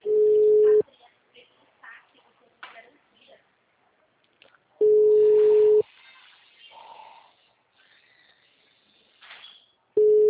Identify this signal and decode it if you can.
Ringback tone (Europe, 425 Hz), with speech.
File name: out- -2056- - - .83482